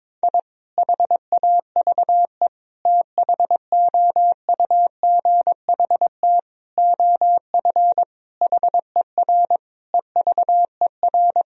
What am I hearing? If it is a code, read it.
Morse code, heard as a keyed tone, about 22 wpm, 700 Hz. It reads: I HAVE THOUGHT OF HER EVER
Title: Sentences from Top 1000 Words - Review Part 1 - 22wpm